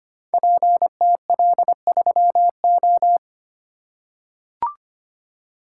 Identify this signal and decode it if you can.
Morse code, heard as a keyed tone, about 25 wpm, 700 Hz. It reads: PTL3O
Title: Code Groups - 5 Letter:Numbers - 25wpm